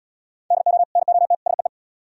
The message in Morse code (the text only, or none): DO NOT SEE